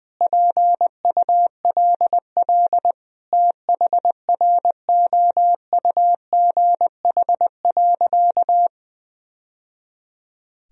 Morse code, heard as a keyed tone, about 20 wpm, 700 Hz. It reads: PULL THROUGH.
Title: Sentences from Top 1000 Words - Review Part 2 - 20wpm